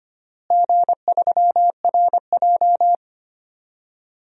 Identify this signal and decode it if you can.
Morse code, heard as a keyed tone, about 25 wpm, 700 Hz. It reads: G3RJ
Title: DXCC-100 Call Signs - 4 Characters - Repeated 2x - 25wpm